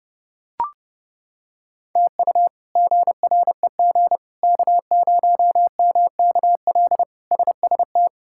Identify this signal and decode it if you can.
Morse code, heard as a keyed tone, about 30 wpm, 700 Hz. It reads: TU GREG K0MKL SST
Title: K1USN SST QSO Practice - 30wpm